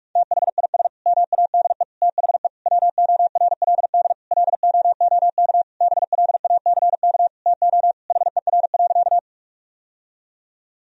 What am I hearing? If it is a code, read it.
Morse code, heard as a keyed tone, about 46 wpm, 700 Hz. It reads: THIS MADE THE WORLD LOOK BLACK TO HER.
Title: Sentences from Top 1000 Words - Review Part 2 - 45wpm